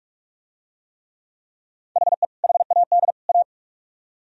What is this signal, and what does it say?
Morse code, heard as a keyed tone, about 45 wpm, 700 Hz. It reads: HE HAD A